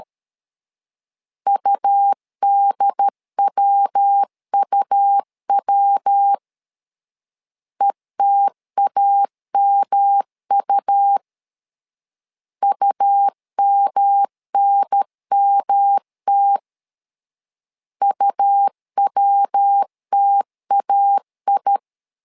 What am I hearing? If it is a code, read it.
Morse code, heard as a keyed tone, about 13 wpm, 800 Hz. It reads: UDWUW ETAMU UMNMT UWTAI
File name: LEZIONI / 3 MWRO / 3 MW 12wpm 1633